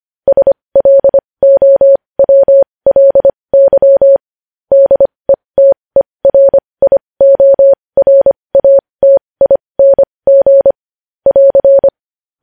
Morse code, heard as a keyed tone, about 25 wpm, spaced as for 20 wpm, 550 Hz. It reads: SLOWLY DETERIORATING <AR>
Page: news-headlines-in-morse-code-at-20-wpm